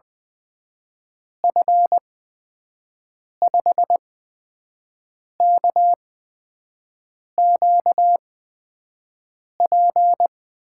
Morse code, heard as a keyed tone, about 20 wpm, spaced as for 5 wpm, 700 Hz. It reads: F5KQP
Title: USA Call Signs - 6 Characters - 8x Character Spacing - 20wpm